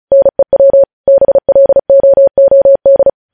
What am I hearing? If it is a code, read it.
Morse code, heard as a keyed tone, about 36 wpm, 550 Hz. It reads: NEW BLOOD
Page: news headlines in morse code at 35 wpm